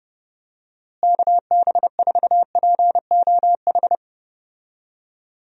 Morse code, heard as a keyed tone, about 31 wpm, 700 Hz. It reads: KB4POH
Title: USA Call Signs - 6 Characters - Repeated 3x - 30wpm